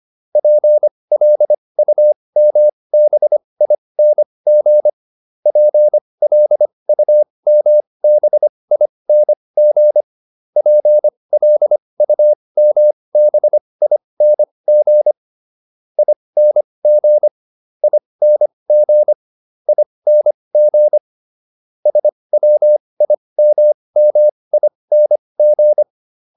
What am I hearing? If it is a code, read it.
Morse code, heard as a keyed tone, about 25 wpm, spaced as for 20 wpm, 600 Hz. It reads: PLUMBING PLUMBING PLUMBING ING ING ING SWIMMING